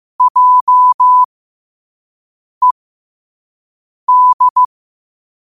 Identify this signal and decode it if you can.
Morse code, heard as a keyed tone, about 15 wpm, spaced as for 5 wpm, 1000 Hz. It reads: JED
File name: 5wpm